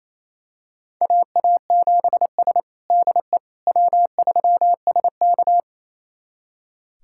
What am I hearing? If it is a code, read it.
Morse code, heard as a keyed tone, about 27 wpm, 700 Hz. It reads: AA7S DE W3SK
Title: USA Callsign DE Callsign - 4 Characters - 28wpm